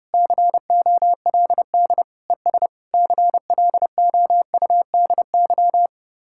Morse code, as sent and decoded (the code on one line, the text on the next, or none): -.-. --- .-.. -.. / . ... / -.-. .-.. --- ..- -.. -.--
COLD ES CLOUDY